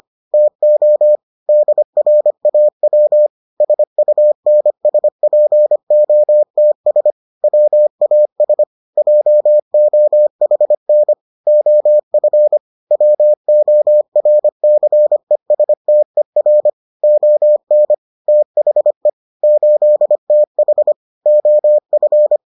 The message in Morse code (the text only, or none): TO DRAW SUNSPOTS WAS JOHN OF WORCESTER ON THE 8TH OF